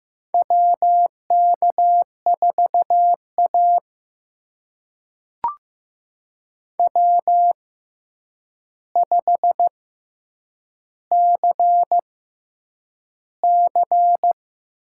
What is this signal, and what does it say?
Morse code, heard as a keyed tone, about 15 wpm, spaced as for 8 wpm, 700 Hz. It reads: WK4A W5CC
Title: USA Call Signs - 4 Characters - 6x Character Spacing - 15wpm